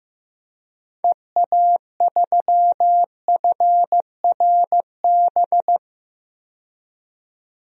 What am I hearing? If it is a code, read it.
Morse code, heard as a keyed tone, about 16 wpm, 700 Hz. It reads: EA3FRB
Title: DXCC-100 Call Signs - 6 Characters - 15wpm